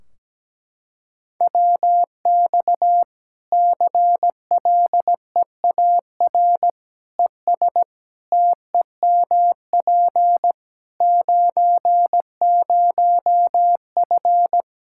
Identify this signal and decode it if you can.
Morse code, heard as a keyed tone, about 17 wpm, 700 Hz. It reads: WX CLEAR ES TEMP 90F
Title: Practice QSO - Exchange 2 - Info - 17wpm